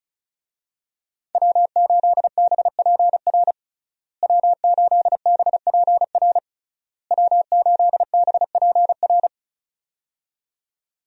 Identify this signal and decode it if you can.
Morse code, heard as a keyed tone, about 36 wpm, 700 Hz. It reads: W8BPR W8BPR W8BPR
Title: USA Call Signs - 5 Characters - Repeated 3x - 35wpm